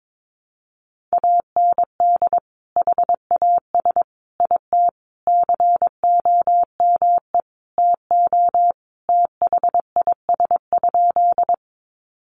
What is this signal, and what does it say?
Morse code, heard as a keyed tone, about 22 wpm, 700 Hz. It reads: AND HAS IT COME TO THIS?
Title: Sentences from Top 100 Words 22wpm